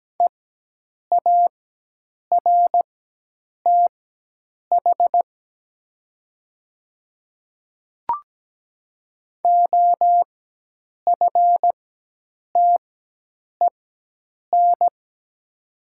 Morse code, heard as a keyed tone, about 17 wpm, spaced as for 8 wpm, 700 Hz. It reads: EARTH OFTEN
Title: 4x Character Spacing - Top 300 Words - 17wpm